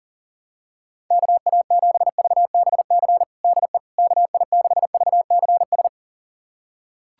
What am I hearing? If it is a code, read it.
Morse code, heard as a keyed tone, about 40 wpm, 700 Hz. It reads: KA7VBC DE KI6VCS
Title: USA Callsign DE Callsign - 6 Characters - 40wpm